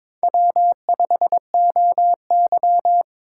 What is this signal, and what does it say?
Morse code, heard as a keyed tone, about 26 wpm, 700 Hz. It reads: W5OY
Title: Speed Racing - USA 1x2 Call Signs 22wpm